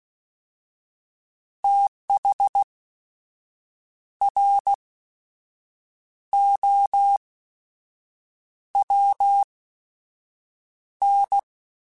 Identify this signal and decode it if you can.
Morse code, heard as a keyed tone, about 16 wpm, spaced as for 6 wpm, 800 Hz. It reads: THROWN
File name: TH / TH mp3s / Fast TH @15